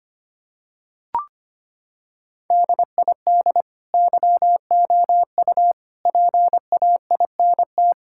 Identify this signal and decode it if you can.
Morse code, heard as a keyed tone, about 25 wpm, 700 Hz. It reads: DID YOU PAINT
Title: Sentences from Top 1000 Words - Review Part 6 - 25wpm